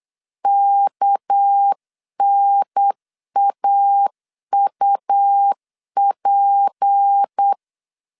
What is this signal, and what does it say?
Morse code, heard as a keyed tone, about 8 wpm, 800 Hz. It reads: KNAUP
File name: CW / DET / 1 6 1500C S C 8wpm 5219